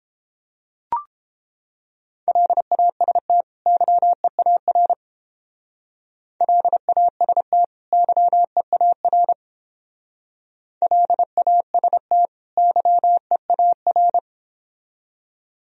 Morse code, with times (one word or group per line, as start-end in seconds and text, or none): 2.3-3.4 LAST
3.7-4.9 YEAR
6.4-7.6 LAST
7.9-9.3 YEAR
10.8-12.3 LAST
12.6-14.2 YEAR